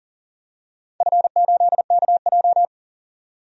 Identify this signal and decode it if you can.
Morse code, heard as a keyed tone, about 40 wpm, 700 Hz. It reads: F8KJ
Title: DXCC-100 Call Signs - 4 Characters - 40wpm